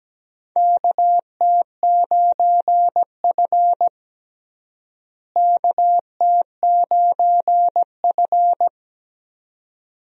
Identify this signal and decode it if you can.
Morse code, heard as a keyed tone, about 17 wpm, 700 Hz. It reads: KT9F KT9F
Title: USA Call Signs - 4 Characters - Repeated 2x - 17wpm